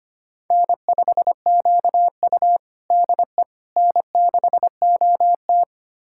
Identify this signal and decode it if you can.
Morse code, heard as a keyed tone, about 25 wpm, 700 Hz. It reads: N5QU DE N6OT
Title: USA Callsign DE Callsign - 4 Characters - 25wpm